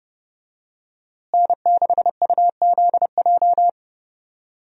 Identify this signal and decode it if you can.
Morse code, heard as a keyed tone, about 30 wpm, 700 Hz. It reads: N6UZJ
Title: USA Call Signs - 5 Characters - Repeated 3x - 30wpm